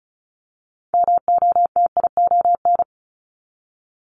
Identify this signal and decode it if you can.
Morse code, heard as a keyed tone, about 36 wpm, 700 Hz. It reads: MOTION
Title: Multi-letter Phonemes 35wpm